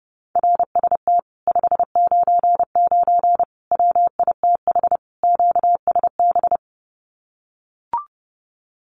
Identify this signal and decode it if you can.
Morse code, heard as a keyed tone, about 32 wpm, 700 Hz. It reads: RST 599 WITH QSB
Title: Common QSO Phrases Encore 30wpm